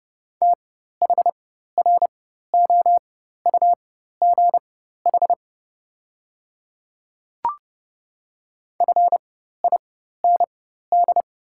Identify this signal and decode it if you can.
Morse code, heard as a keyed tone, about 31 wpm, spaced as for 14 wpm, 690 Hz. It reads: THROUGH FIND
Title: 4x Character Spacing - Top 200 Words - 30wpm